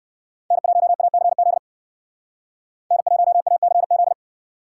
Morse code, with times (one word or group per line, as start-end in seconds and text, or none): 0.5-1.6 N1ACZ
2.9-4.1 N1ACZ